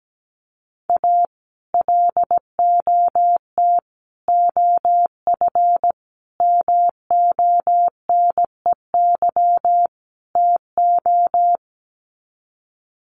Morse code, heard as a keyed tone, about 17 wpm, 700 Hz. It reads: A LOT OF MONEY TO
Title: Sets of 5 Words Encore2 17wpm